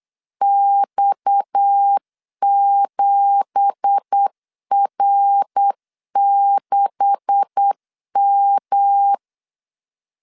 Morse code, heard as a keyed tone, about 8 wpm, 800 Hz. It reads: X7R6M
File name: CW / DET / 1 9 1500C S A 8wpm 5637